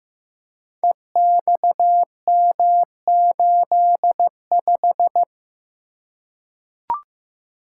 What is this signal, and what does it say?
Morse code, heard as a keyed tone, about 15 wpm, 700 Hz. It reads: EXM85
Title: Code Groups - 5 Letter:Numbers - 15wpm